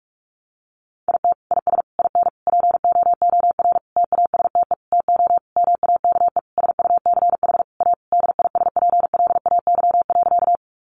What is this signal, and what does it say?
Morse code, heard as a keyed tone, about 44 wpm, 700 Hz. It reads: IT IS IN POOR TASTE TO MAKE SUCH A DISPLAY.
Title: Sentences from Top 1200 Words 45wpm